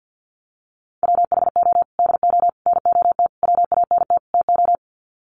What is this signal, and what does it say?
Morse code, heard as a keyed tone, about 50 wpm, 700 Hz. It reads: WHO DO NOT WANT TO
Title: Sets of 5 Words Encore2 50wpm